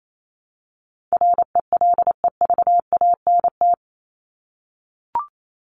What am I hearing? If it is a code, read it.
Morse code, heard as a keyed tone, about 28 wpm, 700 Hz. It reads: RELEVANT
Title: Top 1001-1250 Words 28wpm